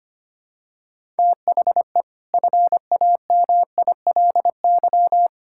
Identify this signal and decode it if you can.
Morse code, heard as a keyed tone, about 25 wpm, 700 Hz. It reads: THE FAMILY
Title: Sentences from Top 1000 Words - Review Part 6 - 25wpm